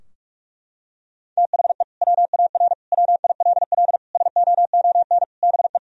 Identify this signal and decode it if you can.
Morse code, heard as a keyed tone, about 46 wpm, 700 Hz. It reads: THE WAR WILL SOON BE
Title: Sentences from Top 1000 Words - Review Part 2 - 45wpm